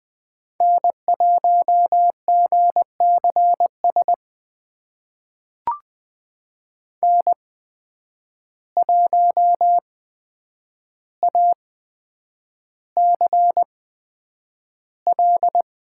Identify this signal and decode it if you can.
Morse code, heard as a keyed tone, about 20 wpm, spaced as for 9 wpm, 700 Hz. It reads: N1GCS N1ACL
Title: USA Call Signs - 5 Characters - 8x Character Spacing - 20wpm